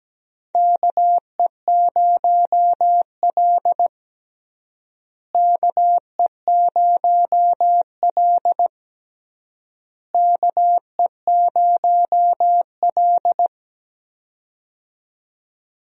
Morse code, heard as a keyed tone, about 17 wpm, 700 Hz. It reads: KE0L KE0L KE0L
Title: USA Call Signs - 4 Characters - Repeated 3x - 17wpm